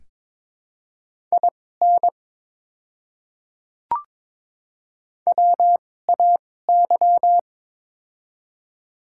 Morse code, heard as a keyed tone, about 22 wpm, spaced as for 16 wpm, 700 Hz. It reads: IN WAY
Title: 2x Character Spacing - Top 100 Words - 22wpm